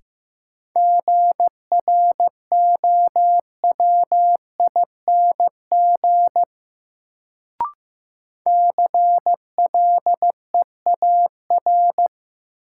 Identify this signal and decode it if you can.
Morse code, heard as a keyed tone, about 15 wpm, 700 Hz. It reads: GROWING CLEAR
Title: Sets of 5 Words Encore - New Word Drill - 15wpm